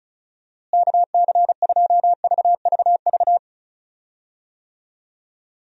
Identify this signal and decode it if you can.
Morse code, heard as a keyed tone, about 34 wpm, 700 Hz. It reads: KC2VVV
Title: USA Call Signs - 6 Characters - Repeated 2x - 35wpm